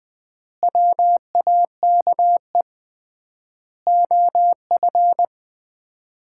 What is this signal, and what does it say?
Morse code, heard as a keyed tone, about 21 wpm, 700 Hz. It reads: WAKE OF
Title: 3x Word-Spacing - Sets of 4 Words - 20wpm